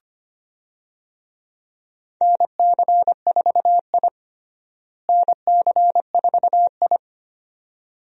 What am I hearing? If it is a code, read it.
Morse code, heard as a keyed tone, about 25 wpm, 700 Hz. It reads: NC4I NC4I
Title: USA Call Signs - 4 Characters - Repeated 2x - 25wpm